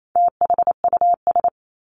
Morse code, heard as a keyed tone, about 28 wpm, 700 Hz. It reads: THUS
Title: Top 751-1000 Words 28wpm